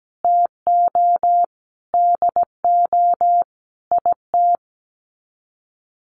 Morse code, heard as a keyed tone, about 17 wpm, 700 Hz. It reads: TO DO IT
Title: Sets of 5 Words Encore 17wpm